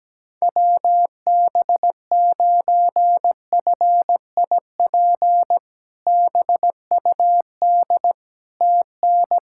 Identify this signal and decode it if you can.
Morse code, heard as a keyed tone, about 17 wpm, 700 Hz. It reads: WB9FIP BUD TN
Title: K1USN SST QSO Practice - 17wpm